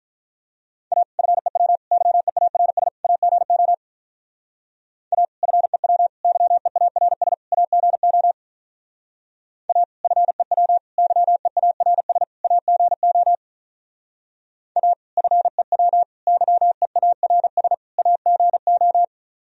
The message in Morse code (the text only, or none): A FEW YEARS AGO A FEW YEARS AGO A FEW YEARS AGO A FEW YEARS AGO